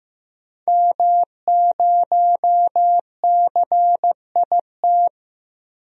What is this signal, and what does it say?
Morse code, heard as a keyed tone, about 15 wpm, 700 Hz. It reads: M0CIT
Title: DXCC-100 Call Signs - 5 Characters - Repeated 3x - 15wpm